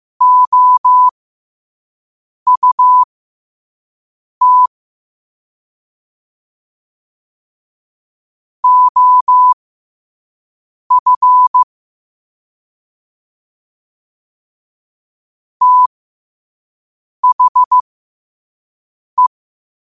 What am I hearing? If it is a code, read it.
Morse code, heard as a keyed tone, about 15 wpm, spaced as for 5 wpm, 1000 Hz. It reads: OUT OF THE
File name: qotdfiles / 5wpm